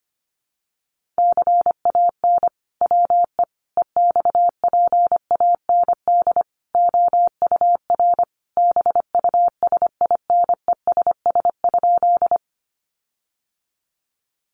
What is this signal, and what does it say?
Morse code, heard as a keyed tone, about 25 wpm, 700 Hz. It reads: CAN WE EXPAND OUR BUSINESS?
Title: Sentences from Top 1400 Words 25wpm